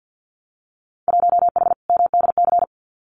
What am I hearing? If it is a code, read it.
Morse code, heard as a keyed tone, about 50 wpm, 700 Hz. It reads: 1H MDC
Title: Winter Field Day Exchanges 50wpm